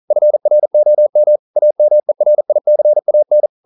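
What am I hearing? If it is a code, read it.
Morse code, heard as a keyed tone, about 42 wpm, 600 Hz. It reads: FROM AMERICAN
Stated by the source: Morse code practice at 41 wpm, from CW med Gnister.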